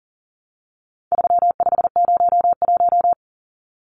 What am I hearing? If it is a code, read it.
Morse code, heard as a keyed tone, about 40 wpm, 700 Hz. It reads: 3501
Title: Numbers - 4-Digit 40wpm